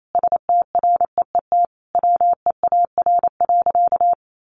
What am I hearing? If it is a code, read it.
Morse code, heard as a keyed tone, about 28 wpm, 700 Hz. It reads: STREET WEAR.
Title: Sentences from Top 1500 Words 28wpm